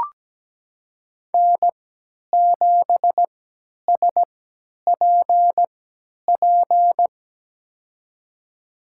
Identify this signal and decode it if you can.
Morse code, heard as a keyed tone, about 17 wpm, spaced as for 10 wpm, 700 Hz. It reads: N7SPP
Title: USA Call Signs - 5 Characters - 3x Character Spacing - 17wpm